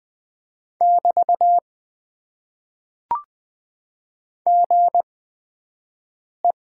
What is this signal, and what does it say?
Morse code, heard as a keyed tone, about 20 wpm, spaced as for 5 wpm, 700 Hz. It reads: <BT> GE
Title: Top 90 QSO Elements - 8x Character Spacing - 20wpm